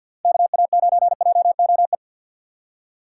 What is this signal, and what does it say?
Morse code, heard as a keyed tone, about 50 wpm, 700 Hz. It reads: KA9JOE
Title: USA Call Signs - 6 Characters - 4x Character Spacing - 50wpm